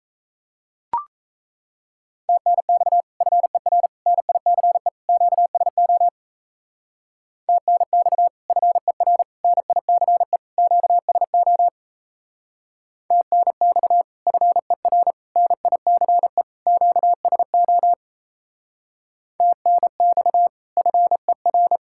Morse code, heard as a keyed tone, about 36 wpm, 700 Hz. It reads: TNX FER NICE QSO TNX FER NICE QSO TNX FER NICE QSO TNX FER